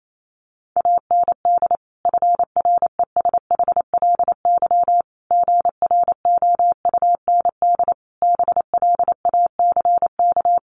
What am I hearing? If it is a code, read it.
Morse code, heard as a keyed tone, about 28 wpm, 700 Hz. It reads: AND FRESHLY GROUND BLACK
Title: Sets of 5 Words Encore 28wpm